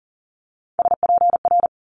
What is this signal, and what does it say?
Morse code, heard as a keyed tone, about 38 wpm, 700 Hz. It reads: SPR
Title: Multi-letter Phonemes Quick Fire 40wpm